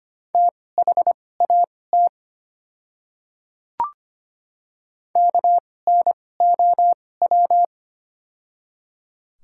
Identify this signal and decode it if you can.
Morse code, heard as a keyed tone, about 25 wpm, spaced as for 18 wpm, 690 Hz. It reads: THAT KNOW
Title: 2x Character Spacing - Top 100 Words - 25wpm